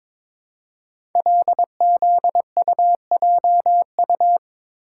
Morse code, heard as a keyed tone, about 22 wpm, 700 Hz. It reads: LZUJU
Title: Code Groups - 5 Letters - 22wpm